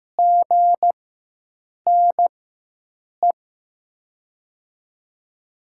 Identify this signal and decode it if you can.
Morse code, heard as a keyed tone, about 15 wpm, spaced as for 7 wpm, 700 Hz. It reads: GNE
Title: USA Call Signs - 6 Characters - 4x Character Spacing - 15wpm